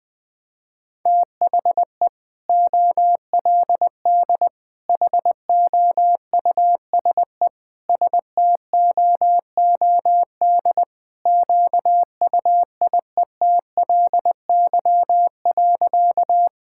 Morse code, heard as a keyed tone, about 20 wpm, 700 Hz. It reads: THE OLD HOUSE STOOD QUIETLY.
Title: Sentences with 2-Word Phrases - Spoken Beginning - 20wpm